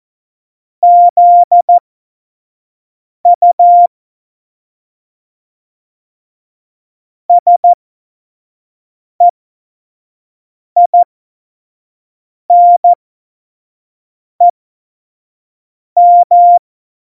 Morse code, heard as a keyed tone, about 14 wpm, spaced as for 5 wpm, 700 Hz. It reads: ZU SEINEM
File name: Morsekurs / Witze / Witz 19. Dezember 2017, 25 F70